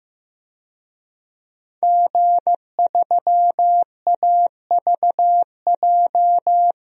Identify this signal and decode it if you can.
Morse code, heard as a keyed tone, about 15 wpm, 700 Hz. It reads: G3AVJ
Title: DXCC-100 Call Signs - 5 Characters - Repeated 2x - 15wpm